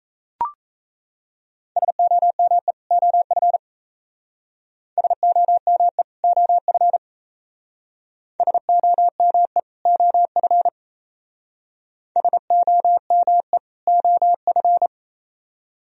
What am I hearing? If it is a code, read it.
Morse code, heard as a keyed tone, about 34 wpm, 700 Hz. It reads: SOME OF SOME OF SOME OF SOME OF
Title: Speed-Racing - Sets of 2 Words 28wpm